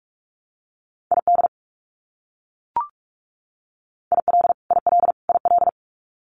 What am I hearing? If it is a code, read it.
Morse code, heard as a keyed tone, about 44 wpm, 700 Hz. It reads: ID IL IL IL
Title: US States Repeated 3x 45wpm